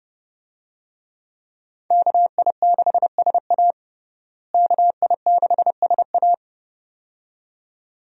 Morse code, heard as a keyed tone, about 31 wpm, 700 Hz. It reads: KI6SA KI6SA
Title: USA Call Signs - 5 Characters - Repeated 2x - 30wpm